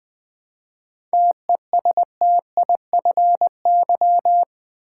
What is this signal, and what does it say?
Morse code, heard as a keyed tone, about 20 wpm, 700 Hz. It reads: TESTIFY
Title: 7 Letter Words - First 3 Letters Given - 20wpm